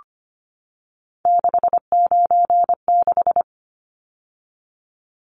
Morse code, heard as a keyed tone, about 25 wpm, 690 Hz. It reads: 696
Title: Numbers - 3-Digit 25wpm